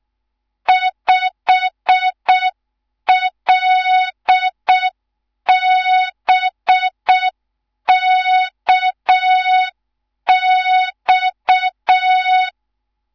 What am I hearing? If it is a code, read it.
Morse code, heard as a keyed tone, about 6 wpm, 3700 Hz. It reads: 5LBKX
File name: Morze-HA2MN / hang / 11 GYAKORLAT